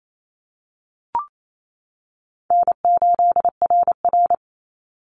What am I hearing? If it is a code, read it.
Morse code, heard as a keyed tone, about 28 wpm, 700 Hz. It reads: N8RR
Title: Call Signs - 28wpm